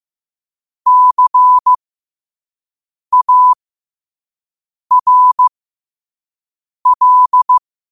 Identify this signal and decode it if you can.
Morse code, heard as a keyed tone, about 15 wpm, spaced as for 5 wpm, 1000 Hz. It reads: CARL
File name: qotdfiles / 5wpm